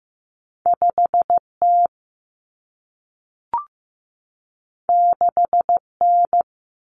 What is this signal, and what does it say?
Morse code, heard as a keyed tone, about 15 wpm, 700 Hz. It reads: 5T 6N
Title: Common Cut Numbers - 2-Digit 15wpm